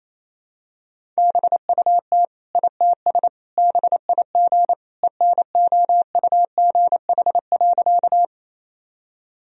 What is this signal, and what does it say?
Morse code, heard as a keyed tone, about 28 wpm, 700 Hz. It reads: BUT ITS BIG ENOUGH.
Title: Sentences from Top 1000 Words - Review Part 1 - 28wpm